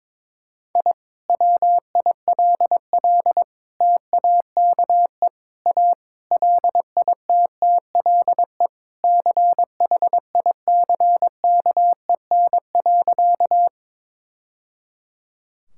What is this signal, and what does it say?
Morse code, heard as a keyed tone, about 22 wpm, 700 Hz. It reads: I WILL TAKE A LITTLE CHICKEN.
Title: Sentences from Top 1000 Words - Review Part 5 - 22wpm